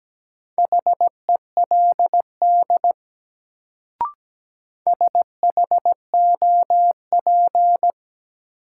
Morse code, heard as a keyed tone, about 17 wpm, 700 Hz. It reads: HELD SHOP
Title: Sentences from Top 1600 Words - New Word Drill - 17wpm